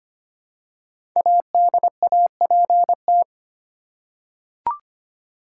Morse code, heard as a keyed tone, about 25 wpm, 700 Hz. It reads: ADAPT
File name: 5 Letter Words - First 3 Letters Given - 25wpm